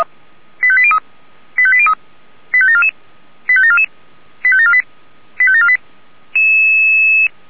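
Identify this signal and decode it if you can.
Five-tone selective call (ZVEI), heard as quick runs of five tones, tones of about 70 ms. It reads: 85803 85803 85744 85744 85748 85748
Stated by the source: ZVEI five-tone alert for code 85725, received on an SDR.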